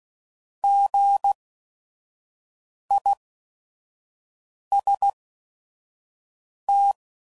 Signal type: Morse code